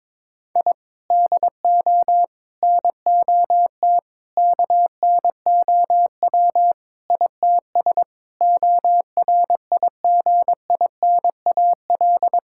Morse code, heard as a keyed tone, about 22 wpm, 700 Hz. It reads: I DO NOT KNOW ITS ORIGINAL
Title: Sentences from Top 1000 Words - Review Part 6 - 22wpm